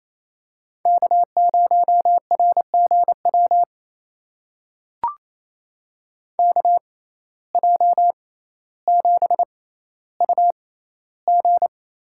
Morse code, heard as a keyed tone, about 28 wpm, spaced as for 14 wpm, 700 Hz. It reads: K0RGW KJ7UG
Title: USA Call Signs - 5 Characters - 6x Character Spacing - 28wpm